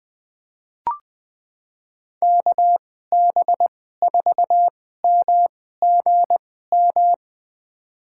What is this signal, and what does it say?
Morse code, heard as a keyed tone, about 20 wpm, spaced as for 14 wpm, 700 Hz. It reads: KB4MGM
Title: USA Call Signs - 6 Characters - 2x Character Spacing - 20wpm